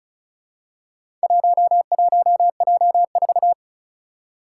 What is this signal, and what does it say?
Morse code, heard as a keyed tone, about 34 wpm, 700 Hz. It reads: 11J4
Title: Four Challenging Letter-Number Combinations 35wpm